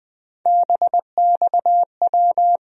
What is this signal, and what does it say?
Morse code, heard as a keyed tone, about 20 wpm, 700 Hz. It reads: BXW
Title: USA Call Sign Postfixes - 3 Letters 20wpm